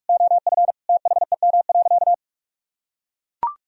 Morse code, heard as a keyed tone, about 41 wpm, 700 Hz. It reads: OF THEM.
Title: Sentences from Top 300 Words and Abbreviations 45wpm